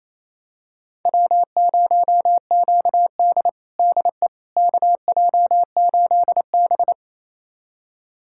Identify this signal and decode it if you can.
Morse code, heard as a keyed tone, about 28 wpm, 690 Hz. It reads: W0QD DE KJ8B
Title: USA Callsign DE Callsign - 4 Characters - 28wpm